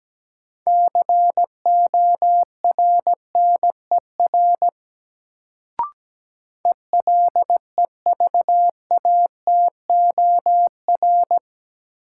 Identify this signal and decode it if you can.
Morse code, heard as a keyed tone, about 17 wpm, 700 Hz. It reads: CORNER ELEVATOR
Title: Sentences from Top 1600 Words - New Word Drill - 17wpm